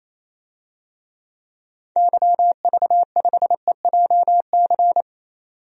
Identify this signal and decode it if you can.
Morse code, heard as a keyed tone, about 28 wpm, 700 Hz. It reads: YV5EJC
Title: DXCC-100 Call Signs - 6 Characters - 28wpm